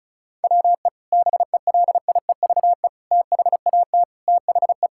Morse code, heard as a keyed tone, about 36 wpm, 700 Hz. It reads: WE BELIEVE THAT THE
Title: Sentences from Top 1000 Words - Review Part 6 - 35wpm